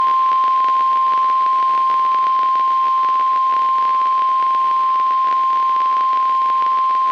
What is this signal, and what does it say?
Emergency Alert System (SAME) tones (the 1050 Hz attention tone). It is heard from 0.4 to 6.2 s.
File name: eas messages / EAS WXM95 Towanda